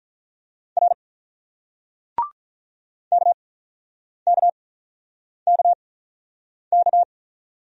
Morse code, heard as a keyed tone, about 47 wpm, spaced as for 9 wpm, 700 Hz. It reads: R KKKK